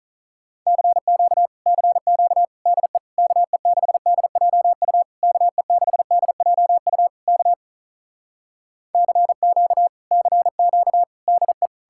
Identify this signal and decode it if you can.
Morse code, heard as a keyed tone, about 46 wpm, 700 Hz. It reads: CQ CQ DE KE6DJU KE6DJU K CQ CQ DE